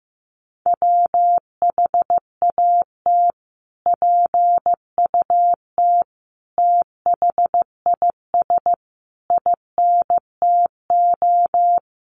Morse code, heard as a keyed tone, about 15 wpm, 700 Hz. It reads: WHAT PUT THIS INTO
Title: Sentences from Top 600 Words 15wpm